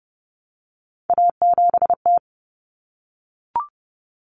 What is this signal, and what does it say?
Morse code, heard as a keyed tone, about 30 wpm, 700 Hz. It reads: A7T
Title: Common Cut Numbers - 3-Digit 30wpm